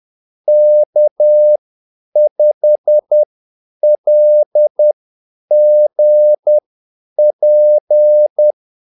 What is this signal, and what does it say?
Morse code, heard as a keyed tone, about 10 wpm, spaced as for 8 wpm, 600 Hz. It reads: K5LGP
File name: Callsigns-10-8wpm